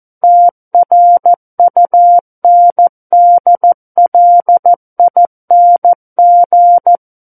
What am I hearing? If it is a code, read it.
Morse code, heard as a keyed tone, about 14 wpm, 700 Hz. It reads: TRUNDLING
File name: Book One / 05-Chapter 5 - 14 WPM